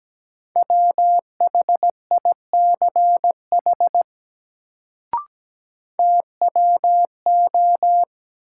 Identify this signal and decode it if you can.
Morse code, heard as a keyed tone, about 17 wpm, 700 Hz. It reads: WHICH TWO
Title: Sentences from Top 100 Words - New Word Drill - 17wpm